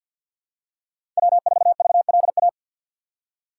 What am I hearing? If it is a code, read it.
Morse code, heard as a keyed tone, about 48 wpm, 700 Hz. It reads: W4VLA
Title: USA Call Signs - 5 Characters - 6x Character Spacing - 50wpm